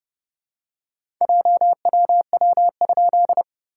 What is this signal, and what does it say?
Morse code, heard as a keyed tone, about 30 wpm, 690 Hz. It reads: JWW?
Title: SOTA QSO Practice 30wpm